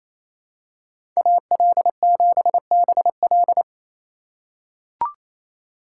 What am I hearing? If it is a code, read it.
Morse code, heard as a keyed tone, about 28 wpm, 700 Hz. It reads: AL7BL
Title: USA Call Signs - 5 Characters - 4x Character Spacing - 28wpm